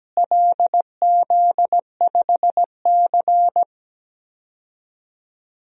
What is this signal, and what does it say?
Morse code, heard as a keyed tone, about 17 wpm, 700 Hz. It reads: LZ5C
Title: DXCC-100 Call Signs - 4 Characters - Repeated 3x - 17wpm